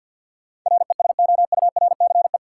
Morse code, heard as a keyed tone, about 51 wpm, 700 Hz. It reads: RESOURCE
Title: Sentences from Top 1300 Words - New Word Drill Alt - 50wpm